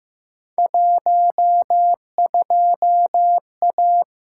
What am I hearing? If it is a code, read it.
Morse code, heard as a keyed tone, about 15 wpm, 700 Hz. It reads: 12A
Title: Field Day Exchanges Encore - 15wpm